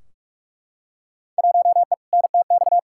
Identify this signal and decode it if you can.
Morse code, heard as a keyed tone, about 44 wpm, 700 Hz. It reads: 1E NTX
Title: Field Day Exchanges - 45wpm